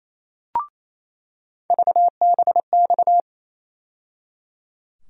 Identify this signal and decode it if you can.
Morse code, heard as a keyed tone, about 28 wpm, 700 Hz. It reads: VBX
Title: USA Call Sign Postfixes - 3 Letters 28wpm